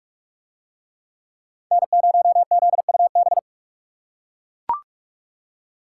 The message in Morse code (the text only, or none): N0ZUB